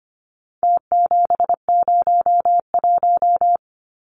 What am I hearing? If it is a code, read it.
Morse code, heard as a keyed tone, about 25 wpm, 700 Hz. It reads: T701
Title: Numbers - Contest Style - 4-Digit 25wpm